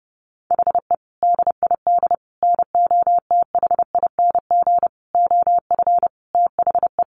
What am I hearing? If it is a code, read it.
Morse code, heard as a keyed tone, about 30 wpm, 700 Hz. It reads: HE DID NOTHING OF THE
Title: Sentences from Top 1300 Words 30wpm